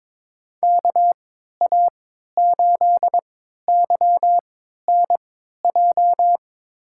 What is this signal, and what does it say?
Morse code, heard as a keyed tone, about 22 wpm, spaced as for 12 wpm, 700 Hz. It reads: KA8YNJ
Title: USA Call Signs - 6 Characters - 3x Character Spacing - 22wpm